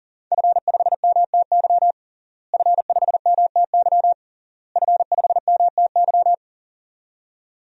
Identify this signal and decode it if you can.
Morse code, heard as a keyed tone, about 40 wpm, 700 Hz. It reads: F5MTY F5MTY F5MTY
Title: DXCC-100 Call Signs - 5 Characters - Repeated 3x - 40wpm